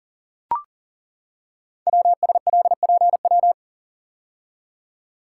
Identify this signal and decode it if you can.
Morse code, heard as a keyed tone, about 40 wpm, 700 Hz. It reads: WSLPW
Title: Code Groups - 5 Letters - 40wpm